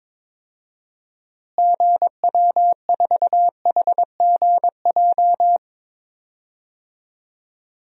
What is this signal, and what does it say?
Morse code, heard as a keyed tone, about 22 wpm, 700 Hz. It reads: GW4HGJ
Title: DXCC-100 Call Signs - 6 Characters - Repeated 2x - 22wpm